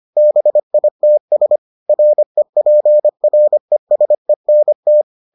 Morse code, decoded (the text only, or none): BITS REPRESENT